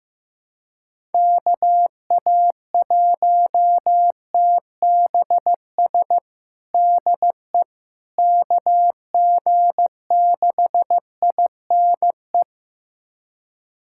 Morse code, heard as a keyed tone, about 15 wpm, 700 Hz. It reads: KA1TBS DE KG6INE
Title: USA Callsign DE Callsign 15wpm